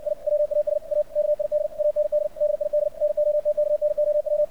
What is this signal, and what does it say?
Morse code, heard as a keyed tone, about 62 wpm, 600 Hz. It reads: I WAS A KID AND DID NOT KNOW
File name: archived / CRSnet